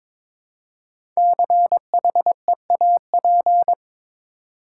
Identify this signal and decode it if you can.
Morse code, heard as a keyed tone, about 22 wpm, 700 Hz. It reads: CHEAP
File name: 5 Letter Words - First 3 Letters Given - 22wpm